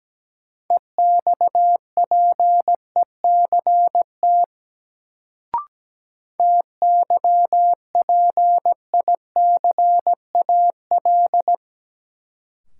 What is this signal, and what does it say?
Morse code, heard as a keyed tone, about 17 wpm, 700 Hz. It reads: EXPECT TYPICAL
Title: Sentences from Top 1100 Words - New Word Drill Alt - 17wpm